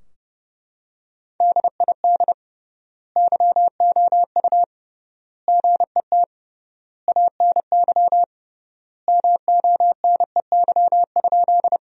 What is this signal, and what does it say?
Morse code, heard as a keyed tone, about 30 wpm, 700 Hz. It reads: DID YOU GET ANY MONEY?
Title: 3x Word-Spacing - Sentences from Top 400 Words - 30wpm